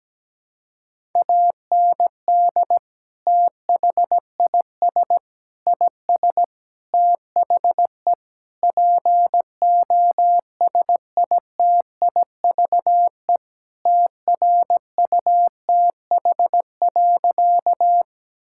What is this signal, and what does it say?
Morse code, heard as a keyed tone, about 17 wpm, 700 Hz. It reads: AND THIS IS THE POSITIVE TRUTH.
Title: Sentences from Top 1000 Words - Review Part 5 - 17wpm